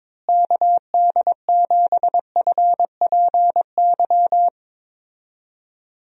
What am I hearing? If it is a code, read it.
Morse code, heard as a keyed tone, about 22 wpm, 700 Hz. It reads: KD7FPY
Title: USA Call Signs - 6 Characters - Repeated 3x - 22wpm